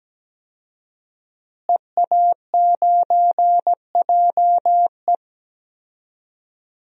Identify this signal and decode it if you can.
Morse code, heard as a keyed tone, about 17 wpm, 700 Hz. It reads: EA9JE
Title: DXCC-100 Call Signs - 5 Characters - Repeated 2x - 17wpm